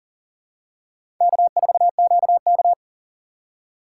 Morse code, heard as a keyed tone, about 40 wpm, 700 Hz. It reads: K4QK
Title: USA Call Signs - 4 Characters - 3x Character Spacing - 40wpm